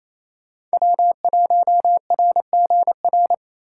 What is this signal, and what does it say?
Morse code, heard as a keyed tone, about 33 wpm, 700 Hz. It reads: W1RGR
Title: Speed Racing - USA 1x3 Call Signs 28wpm